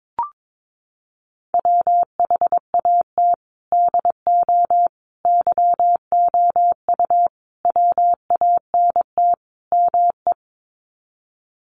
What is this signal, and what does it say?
Morse code, heard as a keyed tone, about 22 wpm, 700 Hz. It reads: WHAT DO YOU WANT ME
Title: Sets of 5 Words Encore2 22wpm